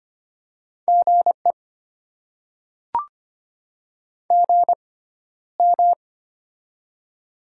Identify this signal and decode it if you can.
Morse code, heard as a keyed tone, about 25 wpm, spaced as for 13 wpm, 690 Hz. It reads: GE GM
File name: Top 20 QSO Elements - 6x Character Spacing - 25wpm